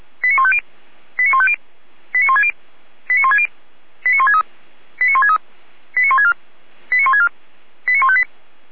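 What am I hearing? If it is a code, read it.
Five-tone selective call (ZVEI), heard as quick runs of five tones, tones of about 80 ms. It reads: 89150 89150 89160 89160 89163 89163 89164 89164 89158
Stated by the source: ZVEI five-tone selective call received on an SDR.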